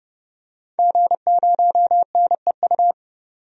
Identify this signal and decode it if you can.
Morse code, heard as a keyed tone, about 30 wpm, 700 Hz. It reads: G0NEU
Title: DXCC-100 Call Signs - 5 Characters - 30wpm